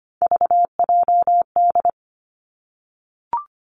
Morse code, heard as a keyed tone, about 25 wpm, 700 Hz. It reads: VJD
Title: Three Challenging Letter-Number Combinations 25wpm